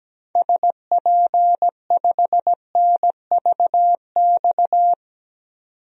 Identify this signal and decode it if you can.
Morse code, heard as a keyed tone, about 17 wpm, 700 Hz. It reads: SP5NVX
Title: DXCC-100 Call Signs - 6 Characters - 17wpm